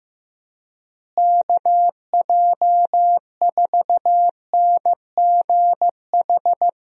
Morse code, heard as a keyed tone, about 15 wpm, 700 Hz. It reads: KJ4NGH
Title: USA Call Signs - 6 Characters - Repeated 2x - 15wpm